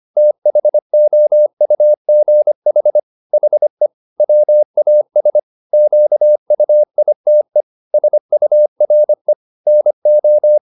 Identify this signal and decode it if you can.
Morse code, heard as a keyed tone, about 25 wpm, 600 Hz. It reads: THOUGH HE WAS QUITE SURE NO
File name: the old man and the sea part 1 25wpm